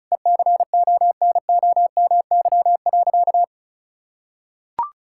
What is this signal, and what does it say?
Morse code, heard as a keyed tone, about 34 wpm, 700 Hz. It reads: ECONOMY.
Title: Sentences from Top 1000 Words - Review Part 5 - 35wpm